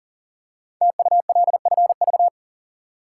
Morse code, heard as a keyed tone, about 43 wpm, 700 Hz. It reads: TULFV
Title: Code Groups - 5 Letters - 40wpm